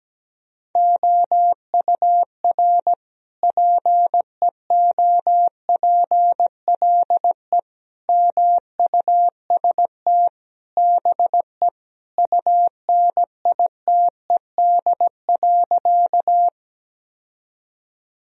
Morse code, heard as a keyed tone, about 17 wpm, 700 Hz. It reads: OUR PEOPLE MUST BE UNITED.
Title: Sentences from Top 1000 Words - Review Part 4 - 17wpm